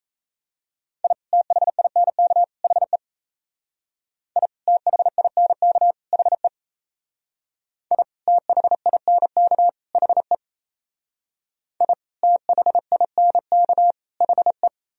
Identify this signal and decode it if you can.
Morse code, heard as a keyed tone, about 34 wpm, 700 Hz. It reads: I THINK HE I THINK HE I THINK HE I THINK HE
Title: Speed-Racing - Sets of 3 Words 28wpm